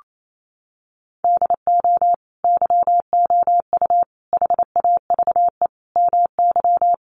Morse code, heard as a keyed tone, about 28 wpm, 700 Hz. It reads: DO YOU HAVE MY
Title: Sentences from Top 500 Words 28wpm